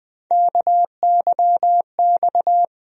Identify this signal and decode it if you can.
Morse code, heard as a keyed tone, about 20 wpm, 700 Hz. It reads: KYX
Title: USA Call Sign Postfixes - 3 Letters 20wpm